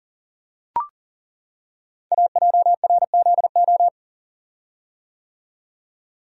Morse code, heard as a keyed tone, about 41 wpm, 690 Hz. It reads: AJRZO
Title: Code Groups - 5 Letter:Numbers - 40wpm